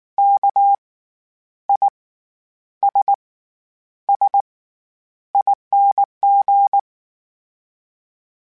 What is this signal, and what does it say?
Morse code, heard as a keyed tone, about 19 wpm, spaced as for 9 wpm, 800 Hz. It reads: KISSING
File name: Special INGs / Special INGs-mp3s / SlowToFaster ing @18